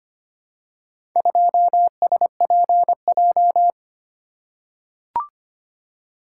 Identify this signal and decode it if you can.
Morse code, heard as a keyed tone, about 25 wpm, 700 Hz. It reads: 2SPJ
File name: practice / 25wpm / Four Letter-Number Combinations - 25wpm